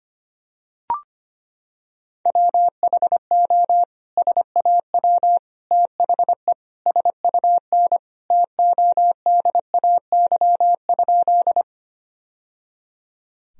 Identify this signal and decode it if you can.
Morse code, heard as a keyed tone, about 25 wpm, 700 Hz. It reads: WHO SAW THE SUN TODAY?
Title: Sentences from Top 1000 Words - Review Part 1 - 25wpm